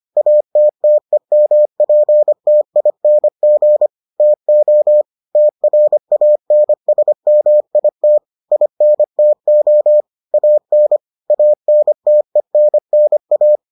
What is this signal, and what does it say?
Morse code, heard as a keyed tone, about 25 wpm, 600 Hz. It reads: ATTEMPTING TO TRANSMIT INTO AN ANTENNA